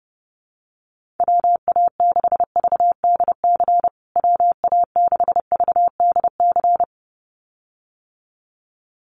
Morse code, heard as a keyed tone, about 30 wpm, 700 Hz. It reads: WA6VDC WA6VDC
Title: USA Call Signs Repeated 2x 30wpm